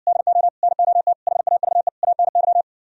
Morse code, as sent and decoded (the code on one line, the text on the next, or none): -.. --- / -. --- - / .... .- ...- . / .- -. -.--
DO NOT HAVE ANY